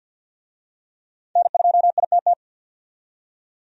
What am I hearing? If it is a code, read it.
Morse code, heard as a keyed tone, about 53 wpm, 700 Hz. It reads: N2ITT
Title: USA Call Signs - 5 Characters - 50wpm